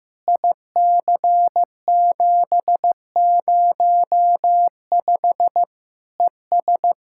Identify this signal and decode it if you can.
Morse code, heard as a keyed tone, about 15 wpm, 700 Hz. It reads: IC705 ES
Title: Practice QSO - Exchange 2 - Info - 15wpm